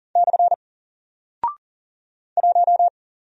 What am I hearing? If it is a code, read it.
Morse code, heard as a keyed tone, about 40 wpm, 700 Hz. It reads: / 1
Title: Single Character (Letter:Number:Punctuation:Prosign) Alt 40wpm